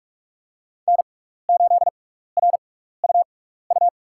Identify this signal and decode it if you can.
Morse code, heard as a keyed tone, about 45 wpm, spaced as for 15 wpm, 700 Hz. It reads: N8RUU
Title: USA Call Signs - 5 Characters - 6x Character Spacing - 45wpm